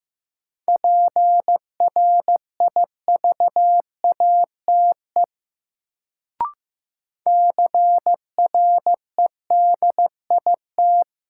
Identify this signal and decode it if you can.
Morse code, heard as a keyed tone, about 15 wpm, 700 Hz. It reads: PRIVATE CREDIT
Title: Sentences from Top 1000 Words - New Word Drill - 15wpm